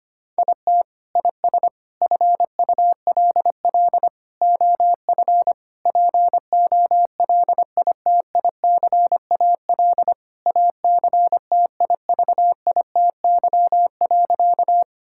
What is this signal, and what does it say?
Morse code, heard as a keyed tone, about 25 wpm, 700 Hz. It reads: IT IS FULL OF POLITICAL ACTIVITY.
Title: Sentences from Top 1000 Words - Review Part 5 - 25wpm